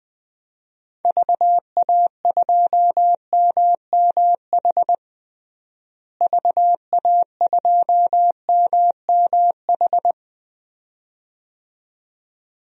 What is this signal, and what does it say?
Morse code, heard as a keyed tone, about 20 wpm, 700 Hz. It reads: VA2MMH VA2MMH
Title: DXCC-100 Call Signs - 6 Characters - Repeated 2x - 20wpm